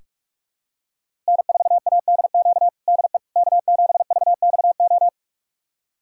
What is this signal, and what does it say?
Morse code, heard as a keyed tone, about 46 wpm, 700 Hz. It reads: N4ADQ DE K7VXO
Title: USA Callsign DE Callsign - 5 Characters - 45wpm